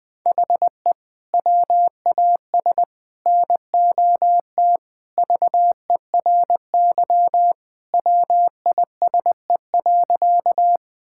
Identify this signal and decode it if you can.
Morse code, heard as a keyed tone, about 21 wpm, 700 Hz. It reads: HE WAS NOT VERY WISE.
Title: Sentences from Top 1600 Words 20wpm